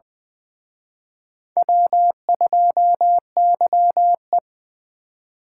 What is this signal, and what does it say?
Morse code, heard as a keyed tone, about 24 wpm, 700 Hz. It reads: W2YE
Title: Speed Racing - USA Call Signs 20wpm